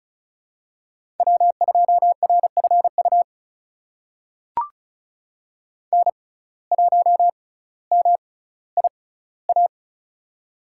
Morse code, heard as a keyed tone, about 36 wpm, spaced as for 18 wpm, 700 Hz. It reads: W2RFU N1MIA
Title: USA Call Signs - 5 Characters - 6x Character Spacing - 35wpm